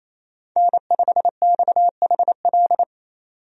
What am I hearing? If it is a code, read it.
Morse code, heard as a keyed tone, about 28 wpm, 700 Hz. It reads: N5XHL
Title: USA Call Signs - 5 Characters - 28wpm